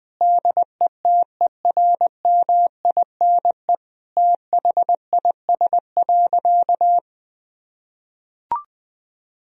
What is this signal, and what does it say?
Morse code, heard as a keyed tone, about 21 wpm, 700 Hz. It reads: DETERMINE THIS.
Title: Sentences from Top 1000 Words - Review Part 4 - 20wpm